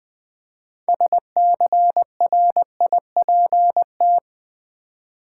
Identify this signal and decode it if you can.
Morse code, heard as a keyed tone, about 20 wpm, 700 Hz. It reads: SCRIPT
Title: Six Letter Words - 20wpm